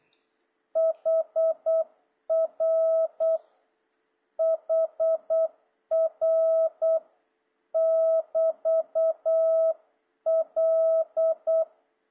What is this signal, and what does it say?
Morse code, heard as a keyed tone, about 8 wpm, 660 Hz. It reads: HR HR <BT> L
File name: jh8cbh / 2026cw